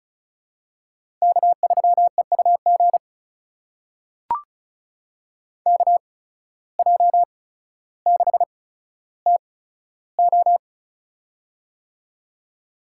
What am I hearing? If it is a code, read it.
Morse code, heard as a keyed tone, about 36 wpm, spaced as for 15 wpm, 700 Hz. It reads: K3EUG KJ6TO
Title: USA Call Signs - 5 Characters - 8x Character Spacing - 35wpm